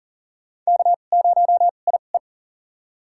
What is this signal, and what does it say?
Morse code, heard as a keyed tone, about 40 wpm, spaced as for 29 wpm, 700 Hz. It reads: K0IE
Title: USA Call Signs - 2x Character Spacing - 40wpm